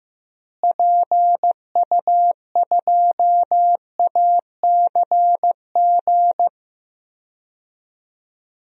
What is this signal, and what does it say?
Morse code, heard as a keyed tone, about 16 wpm, 700 Hz. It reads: PU2ACG
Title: DXCC-100 Call Signs - 6 Characters - Repeated 2x - 15wpm